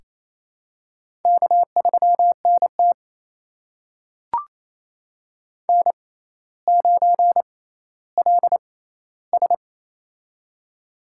Morse code, heard as a keyed tone, about 28 wpm, spaced as for 14 wpm, 700 Hz. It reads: K3NT N9LS
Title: USA Call Signs - 6x Character Spacing - 28wpm